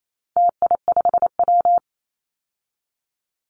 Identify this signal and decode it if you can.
Morse code, heard as a keyed tone, about 28 wpm, 700 Hz. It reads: TI5W
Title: Call Signs Repeated 2x - 28wpm